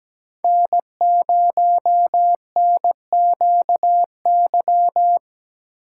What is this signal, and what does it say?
Morse code, heard as a keyed tone, about 17 wpm, 700 Hz. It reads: N0NQY
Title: USA Call Signs - 4x Character Spacing - 17wpm